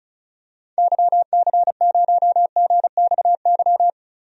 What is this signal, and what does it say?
Morse code, heard as a keyed tone, about 34 wpm, 700 Hz. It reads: YC0GXY
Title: DXCC-100 Call Signs - 6 Characters - 35wpm